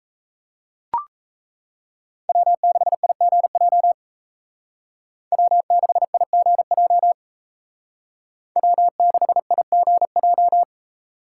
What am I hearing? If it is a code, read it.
Morse code, heard as a keyed tone, about 34 wpm, 700 Hz. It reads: W6IGJ W6IGJ W6IGJ
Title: Speed Racing - USA 1x3 Call Signs 28wpm